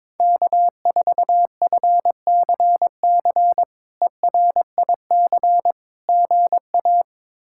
Morse code, heard as a keyed tone, about 22 wpm, 700 Hz. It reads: K4FCC ERIC GA
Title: K1USN SST QSO Practice - 22wpm